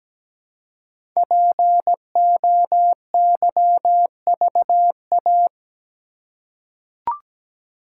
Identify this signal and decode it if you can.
Morse code, heard as a keyed tone, about 17 wpm, 700 Hz. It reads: POYVA
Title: Code Groups - 5 Letters - 17wpm